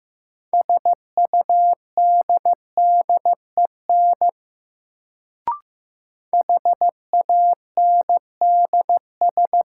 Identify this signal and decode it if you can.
Morse code, heard as a keyed tone, about 15 wpm, 700 Hz. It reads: SUDDEN HANDS
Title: Sets of 4 Words - New Word Drill - 15wpm